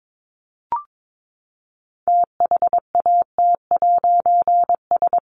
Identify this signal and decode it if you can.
Morse code, heard as a keyed tone, about 22 wpm, 700 Hz. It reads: THAT'S
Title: A Sound Of Thunder 22wpm